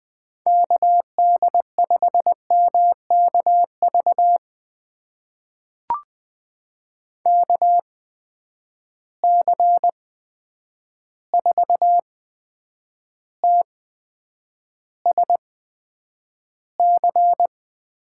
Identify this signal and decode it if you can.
Morse code, heard as a keyed tone, about 20 wpm, spaced as for 9 wpm, 700 Hz. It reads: KD5MKV KC4TSC
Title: USA Call Signs - 6 Characters - 8x Character Spacing - 20wpm